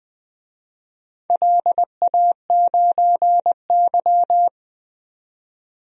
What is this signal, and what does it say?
Morse code, heard as a keyed tone, about 20 wpm, 700 Hz. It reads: LA9Y
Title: DXCC-100 Call Signs - 4 Characters - 20wpm